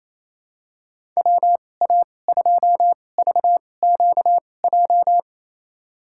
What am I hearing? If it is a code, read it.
Morse code, heard as a keyed tone, about 28 wpm, spaced as for 20 wpm, 700 Hz. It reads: WA2VQJ
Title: USA Call Signs - 6 Characters - 2x Character Spacing - 28wpm